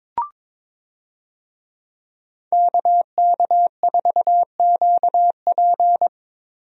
Morse code, heard as a keyed tone, about 22 wpm, 700 Hz. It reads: KK4QP
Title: USA Call Signs - 5 Characters - Repeated 3x - 22wpm